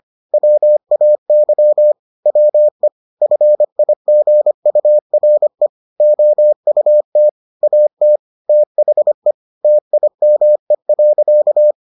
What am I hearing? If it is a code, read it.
Morse code, heard as a keyed tone, about 25 wpm, 600 Hz. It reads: WAY WE FIGURE OUT AT THE TIME.